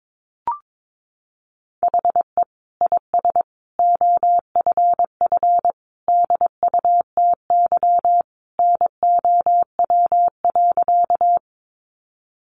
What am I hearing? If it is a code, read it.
Morse code, heard as a keyed tone, about 22 wpm, 700 Hz. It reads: HE IS OFF DUTY NOW.
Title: Sentences from Top 1500 Words 22wpm